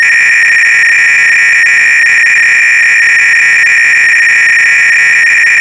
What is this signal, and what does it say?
300-baud modem data.